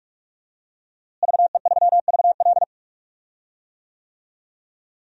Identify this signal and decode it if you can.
Morse code, heard as a keyed tone, about 46 wpm, 700 Hz. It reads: VE3VL
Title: DXCC-100 Call Signs - 5 Characters - Repeated 2x - 45wpm